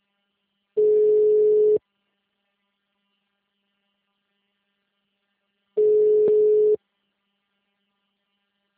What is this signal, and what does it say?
Ringback tone (Europe, 425 Hz).